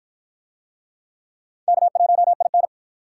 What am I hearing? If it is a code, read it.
Morse code, heard as a keyed tone, about 48 wpm, 700 Hz. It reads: K1IN